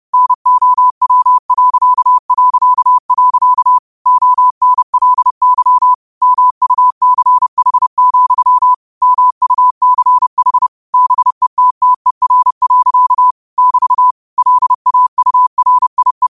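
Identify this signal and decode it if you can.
Morse code, heard as a keyed tone, about 30 wpm, 1000 Hz. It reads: NOW... ONLY MUCH, MUCH BETTER. <BT> LAURIE
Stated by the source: Morse code quotes of the day (QOTD) sent at 30 words per minute.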